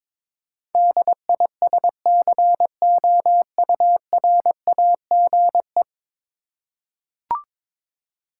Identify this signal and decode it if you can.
Morse code, heard as a keyed tone, about 22 wpm, 700 Hz. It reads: DISCOURAGE
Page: morse-code-ninja-practice-22wpm